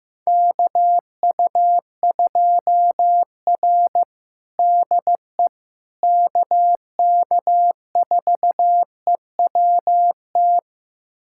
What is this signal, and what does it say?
Morse code, heard as a keyed tone, about 15 wpm, 700 Hz. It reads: KU2R DE KK4EWT
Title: USA Callsign DE Callsign 15wpm